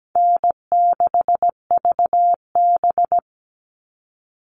Morse code, heard as a keyed tone, about 17 wpm, 700 Hz. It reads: N6VB
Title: All Cut Numbers - 4-Digit 17wpm